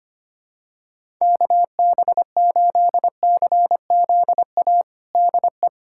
Morse code, heard as a keyed tone, about 25 wpm, 700 Hz. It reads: KB8CZA DE
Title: USA Callsign DE Callsign 25wpm